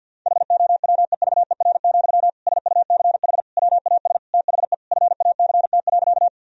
Morse code, heard as a keyed tone, about 50 wpm, 700 Hz. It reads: HOWEVER, SUCH WAS THE FACT.
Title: Sentences from Top 1000 Words - Review Part 3 - 50wpm